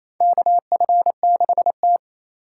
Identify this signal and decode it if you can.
Morse code, heard as a keyed tone, about 28 wpm, 700 Hz. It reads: KF6T
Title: K1USN SST QSO Practice - 28wpm